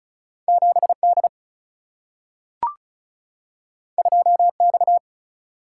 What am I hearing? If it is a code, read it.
Morse code, heard as a keyed tone, about 36 wpm, 700 Hz. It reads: 7 D 2 X
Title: Single-Letter Single-Number 35wpm